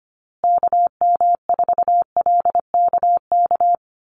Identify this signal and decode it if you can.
Morse code, heard as a keyed tone, about 25 wpm, 700 Hz. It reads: KM4LKK
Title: USA Call Signs Repeated 2x 25wpm